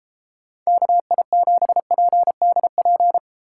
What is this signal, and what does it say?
Morse code, heard as a keyed tone, about 31 wpm, 700 Hz. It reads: KI7PDP